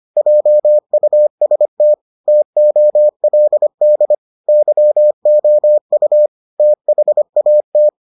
Morse code, heard as a keyed tone, about 25 wpm, 600 Hz. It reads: JUST TOLD YOU THAT